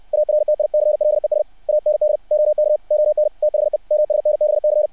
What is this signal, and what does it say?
Morse code, heard as a keyed tone, about 60 wpm, 600 Hz. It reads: ARTICLES AND OK ON THE MSNBC
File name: archived / CRSnet